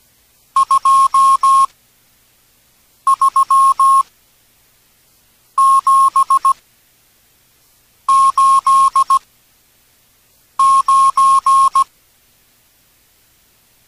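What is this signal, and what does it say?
Morse code, heard as a keyed tone, about 17 wpm, spaced as for 5 wpm, 1080 Hz. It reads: 23789
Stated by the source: Morse from an experimental text-to-speech project for radio amateurs.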